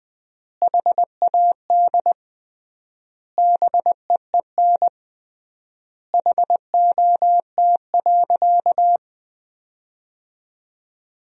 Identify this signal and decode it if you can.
Morse code, heard as a keyed tone, about 20 wpm, 700 Hz. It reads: HAD BEEN HOT.
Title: 3x Word-Spacing - Sentences from Top 500 Words - 20wpm